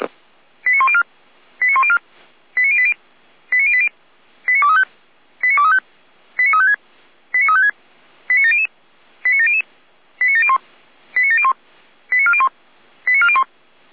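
Five-tone selective call (ZVEI), heard as quick runs of five tones, tones of about 70 ms. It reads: 89104 89104 89080 89080 89235 89235 89356 89356 89700 89700 89701 89701 89401 89401
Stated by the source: ZVEI five-tone selective call received on an SDR.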